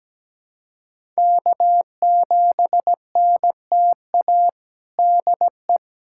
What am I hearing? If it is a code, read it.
Morse code, heard as a keyed tone, about 17 wpm, 700 Hz. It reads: K7NTA DE
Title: USA Callsign DE Callsign - 5 Characters - 17wpm